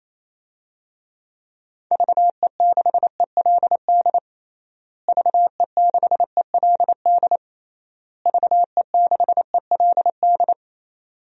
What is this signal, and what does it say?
Morse code, heard as a keyed tone, about 28 wpm, 700 Hz. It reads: VE6ELD VE6ELD VE6ELD
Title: DXCC-100 Call Signs - 6 Characters - Repeated 3x - 28wpm